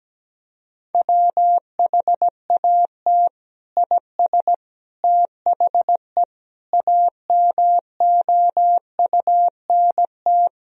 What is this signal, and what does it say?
Morse code, heard as a keyed tone, about 17 wpm, 700 Hz. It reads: WHAT IS THE AMOUNT
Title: Sentences from Top 1000 Words - Review Part 4 - 17wpm